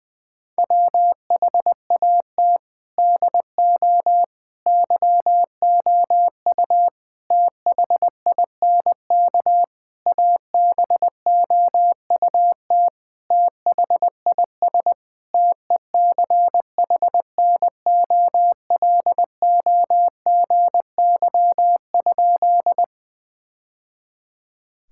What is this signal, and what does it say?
Morse code, heard as a keyed tone, about 20 wpm, 700 Hz. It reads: WHAT DO YOU THINK ABOUT THIS TECHNOLOGY?
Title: Sentences from Top 1000 Words - Review Part 5 - 20wpm